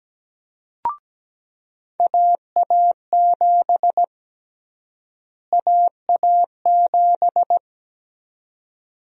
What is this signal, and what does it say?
Morse code, heard as a keyed tone, about 17 wpm, 700 Hz. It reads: AA7 AA7
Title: USA Call Sign Prefixes - Repeated 2x - 17wpm